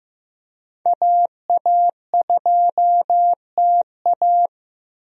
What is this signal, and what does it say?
Morse code, heard as a keyed tone, about 15 wpm, 700 Hz. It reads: AA2TA
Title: USA Call Signs - 5 Characters - 15wpm